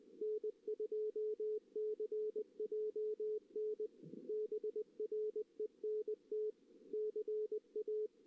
Morse code, heard as a keyed tone, about 19 wpm, 440 Hz. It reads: N2CJN BRENT CA
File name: 2016 JAN CW NAQP / 2016 Jan naqp k5ot - 0-2